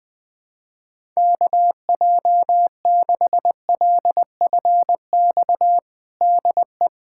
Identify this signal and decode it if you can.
Morse code, heard as a keyed tone, about 20 wpm, 700 Hz. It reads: KJ6LFX DE
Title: USA Callsign DE Callsign - 6 Characters - 20wpm